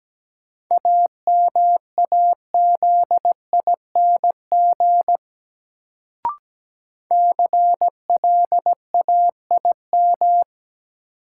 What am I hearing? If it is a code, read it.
Morse code, heard as a keyed tone, about 17 wpm, 700 Hz. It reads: AMAZING CLAIM
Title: Sentences from Top 1500 Words - New Word Drill - 17wpm